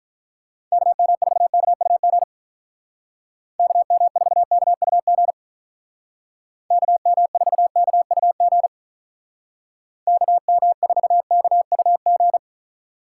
Morse code, heard as a keyed tone, about 43 wpm, 700 Hz. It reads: KM4KUG KM4KUG KM4KUG KM4KUG
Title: Speed Racing - USA 2x3 Call Signs 35wpm